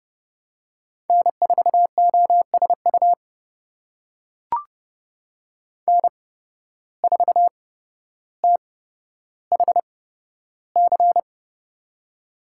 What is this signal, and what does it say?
Morse code, heard as a keyed tone, about 32 wpm, spaced as for 13 wpm, 700 Hz. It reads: N4OSU N4THC
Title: USA Call Signs - 5 Characters - 8x Character Spacing - 30wpm